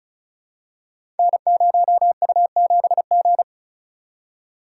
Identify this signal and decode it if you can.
Morse code, heard as a keyed tone, about 36 wpm, 700 Hz. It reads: N0U7G
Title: Code Groups - 5 Letter:Numbers - 35wpm